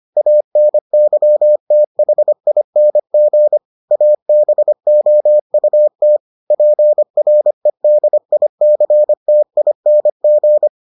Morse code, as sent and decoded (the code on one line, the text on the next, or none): .- -. -.-- - .... .. -. --. / .- -... --- ..- - / .--. .-. -.. .. -.-. - .. -. --.
ANYTHING ABOUT PREDICTING